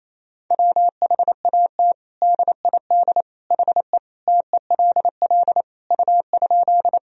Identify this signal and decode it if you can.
Morse code, heard as a keyed tone, about 28 wpm, 700 Hz. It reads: WHAT DID HE TELL U?
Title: Sentences from Top 200 Words and Abbreviations 28wpm